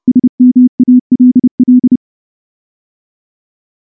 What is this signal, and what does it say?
Morse code, heard as a keyed tone, about 30 wpm, 250 Hz. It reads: SMALL
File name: morse / morse-training-30WPM